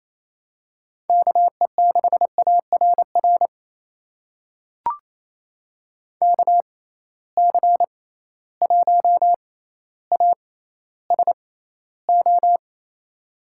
Morse code, heard as a keyed tone, about 28 wpm, spaced as for 14 wpm, 700 Hz. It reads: KE6ARR KC1ASO
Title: USA Call Signs - 6 Characters - 6x Character Spacing - 28wpm